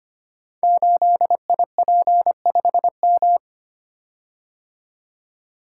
Morse code, heard as a keyed tone, about 25 wpm, 700 Hz. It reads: 8IP5M
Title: Code Groups - 5 Letter:Numbers - 25wpm